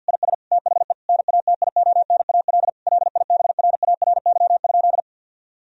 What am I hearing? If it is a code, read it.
Morse code, heard as a keyed tone, about 50 wpm, 700 Hz. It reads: IS THE NATIONAL LIBRARY?
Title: Sentences from Top 1000 Words - Review Part 4 - 50wpm